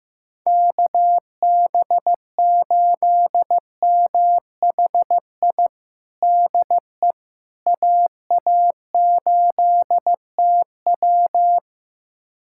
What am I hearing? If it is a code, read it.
Morse code, heard as a keyed tone, about 16 wpm, 700 Hz. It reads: KB8MHI DE AA8TW
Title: USA Callsign DE Callsign 15wpm